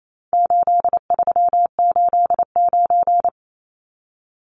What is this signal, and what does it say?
Morse code, heard as a keyed tone, about 28 wpm, 700 Hz. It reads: 8389
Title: Numbers - 4-Digit 28wpm